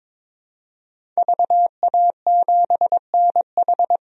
Morse code, heard as a keyed tone, about 22 wpm, 700 Hz. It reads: VA7NH
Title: DXCC-100 Call Signs - 5 Characters - 22wpm